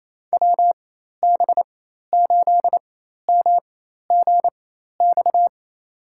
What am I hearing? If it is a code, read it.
Morse code, heard as a keyed tone, about 28 wpm, spaced as for 13 wpm, 700 Hz. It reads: WB8MGX
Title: USA Call Signs - 4x Character Spacing - 28wpm